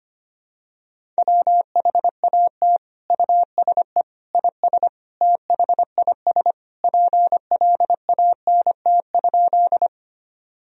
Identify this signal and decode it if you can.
Morse code, heard as a keyed tone, about 25 wpm, 700 Hz. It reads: WHAT USE IS THIS PLANT?
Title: Sentences from Top 1000 Words - Review Part 1 - 25wpm